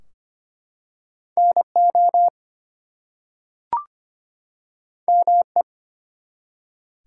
Morse code, heard as a keyed tone, about 25 wpm, 700 Hz. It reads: NO ME
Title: Two Letter Words - 25wpm